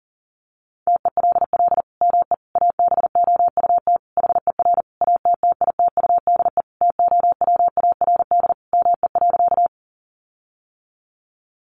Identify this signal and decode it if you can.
Morse code, heard as a keyed tone, about 40 wpm, 700 Hz. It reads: TELL ME ABOUT HER ATTITUDE TOWARD ME.
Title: Sentences from Top 1300 Words 40wpm